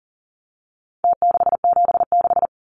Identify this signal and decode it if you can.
Morse code, heard as a keyed tone, about 40 wpm, 700 Hz. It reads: T676
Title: Numbers - Contest Style - 4-Digit 40wpm